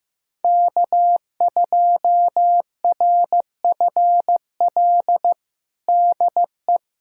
Morse code, heard as a keyed tone, about 16 wpm, 700 Hz. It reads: K2RFL DE
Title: USA Callsign DE Callsign - 5 Characters - 15wpm